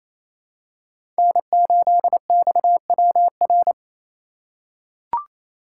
Morse code, heard as a keyed tone, about 28 wpm, 700 Hz. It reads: N8XWR
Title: USA Call Signs - 2x Character Spacing - 28wpm